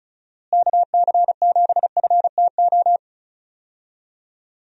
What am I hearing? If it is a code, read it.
Morse code, heard as a keyed tone, about 34 wpm, 700 Hz. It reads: KC7FTO
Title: USA Call Signs - 6 Characters - Repeated 2x - 35wpm